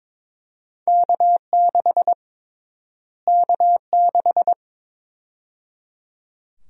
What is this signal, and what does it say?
Morse code, heard as a keyed tone, about 22 wpm, 700 Hz. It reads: K6 K6
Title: USA Call Sign Prefixes - Repeated 2x - 22wpm